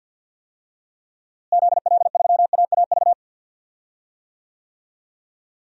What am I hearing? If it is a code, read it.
Morse code, heard as a keyed tone, about 47 wpm, 700 Hz. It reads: ZL3AAV
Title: DXCC-100 Call Signs - 6 Characters - Repeated 2x - 50wpm